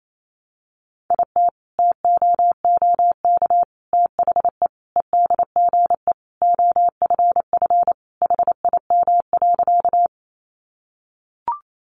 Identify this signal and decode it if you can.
Morse code, heard as a keyed tone, about 28 wpm, 700 Hz. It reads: IT TOOK THE EDGE OFF HIM.
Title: Sentences from Top 1300 Words 28wpm